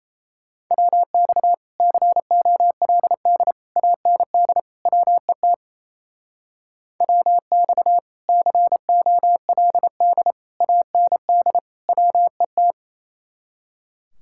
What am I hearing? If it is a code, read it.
Morse code, heard as a keyed tone, about 34 wpm, 700 Hz. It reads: WX COLD AND WET WX COLD AND WET
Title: Speed-Racing Common QSO Phrases - 28wpm